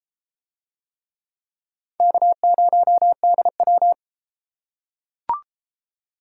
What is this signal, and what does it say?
Morse code, heard as a keyed tone, about 30 wpm, 700 Hz. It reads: K0DW